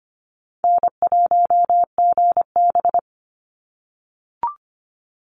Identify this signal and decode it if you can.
Morse code, heard as a keyed tone, about 25 wpm, 700 Hz. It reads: N1GB
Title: NA SOTA Chaser Callsigns 25wpm